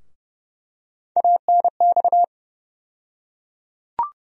Morse code, heard as a keyed tone, about 30 wpm, 700 Hz. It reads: ANX
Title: USA Call Sign Postfixes - 3 Letters 30wpm